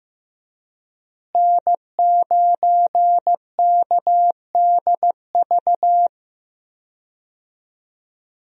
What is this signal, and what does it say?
Morse code, heard as a keyed tone, about 15 wpm, 700 Hz. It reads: N9KDV
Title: USA Call Signs - 5 Characters - Repeated 2x - 15wpm